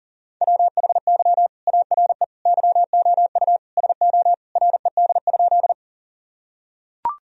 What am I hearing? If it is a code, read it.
Morse code, heard as a keyed tone, about 40 wpm, 700 Hz. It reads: WHY ARE YOU SO RED?
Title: Sentences from Top 1000 Words - Review Part 2 - 40wpm